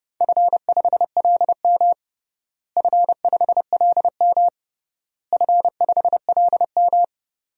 Morse code, heard as a keyed tone, about 32 wpm, 700 Hz. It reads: F5LM F5LM F5LM
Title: DXCC-100 Call Signs - 4 Characters - Repeated 3x - 30wpm